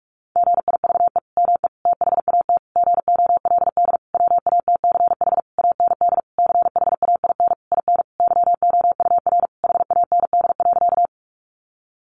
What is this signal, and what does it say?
Morse code, heard as a keyed tone, about 44 wpm, 700 Hz. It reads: GIVE ME THAT GOLD WATCH AND CHAIN IN YOUR HAND.
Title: Sentences from Top 1500 Words 45wpm